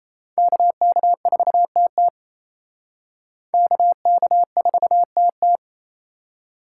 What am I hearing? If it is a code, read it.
Morse code, heard as a keyed tone, about 37 wpm, 700 Hz. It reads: KK4TT KK4TT